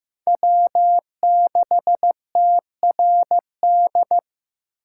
Morse code, heard as a keyed tone, about 15 wpm, 700 Hz. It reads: W6TRD
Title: USA Call Signs - 5 Characters - 8x Character Spacing - 15wpm